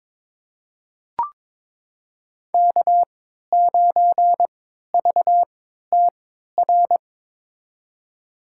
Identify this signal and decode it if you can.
Morse code, heard as a keyed tone, about 22 wpm, spaced as for 12 wpm, 700 Hz. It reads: K9VTR
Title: USA Call Signs - 5 Characters - 3x Character Spacing - 22wpm